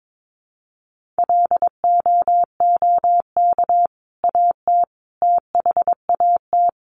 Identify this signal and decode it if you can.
Morse code, heard as a keyed tone, about 22 wpm, 700 Hz. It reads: LOOK AT THAT
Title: Sentences from Top 1100 Words 22wpm